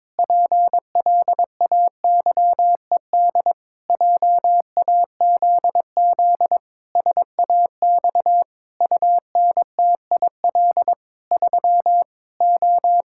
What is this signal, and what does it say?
Morse code, heard as a keyed tone, about 22 wpm, 700 Hz. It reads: PLAYED JAZZ SAX UNTIL 3 O
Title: Pangram Challenge v2 - 22wpm